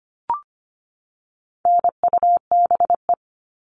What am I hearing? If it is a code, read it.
Morse code, heard as a keyed tone, about 25 wpm, 700 Hz. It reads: NUBE
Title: All Cut Numbers - 4-Digit 25wpm